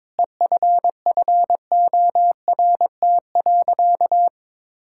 Morse code, heard as a keyed tone, about 22 wpm, 700 Hz. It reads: EFFORT.
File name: practice / 22wpm / Sentences with 8-Word Phrases - 22wpm